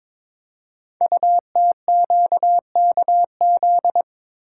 Morse code, heard as a keyed tone, about 22 wpm, 700 Hz. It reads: UTQKZ
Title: Code Groups - 5 Letters - 22wpm